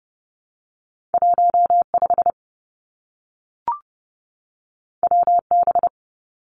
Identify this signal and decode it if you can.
Morse code, heard as a keyed tone, about 30 wpm, 700 Hz. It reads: 15 WB
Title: Two Challenging Letter-Number Combinations 30wpm